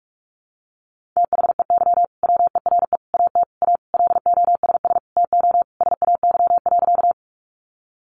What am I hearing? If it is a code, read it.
Morse code, heard as a keyed tone, about 44 wpm, 700 Hz. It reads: THEY WERE AT A LOSS TO SAY.
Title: Sentences from Top 1400 Words 45wpm